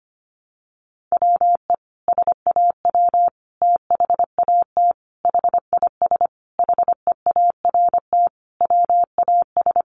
Morse code, heard as a keyed tone, about 25 wpm, 700 Hz. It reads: WE SAW THAT HIS HEART WAS
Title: Sentences from Top 1200 Words 25wpm